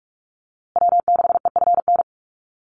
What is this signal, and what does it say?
Morse code, heard as a keyed tone, about 44 wpm, 700 Hz. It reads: W6EFN
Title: USA Call Signs Repeated 3x 45wpm